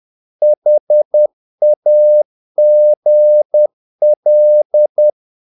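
Morse code, heard as a keyed tone, about 10 wpm, 600 Hz. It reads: HAGL
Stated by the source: Morse practice of short Danish words at 10 wpm, from CW med Gnister.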